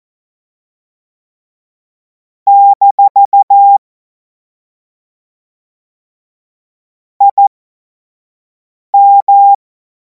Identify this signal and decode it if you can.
Morse code, heard as a keyed tone, about 14 wpm, spaced as for 5 wpm, 800 Hz. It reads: - IM
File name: Morsekurs / Witze / Witz 28. April 2026 25 F70